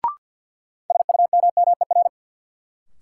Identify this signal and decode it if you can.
Morse code, heard as a keyed tone, about 53 wpm, 700 Hz. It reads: SUMMER
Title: Sentences from Top 1100 Words - New Word Drill Alt - 50wpm